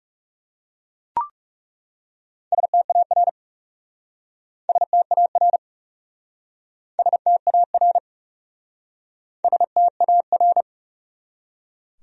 Morse code, heard as a keyed tone, about 40 wpm, 700 Hz. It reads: STAR STAR STAR STAR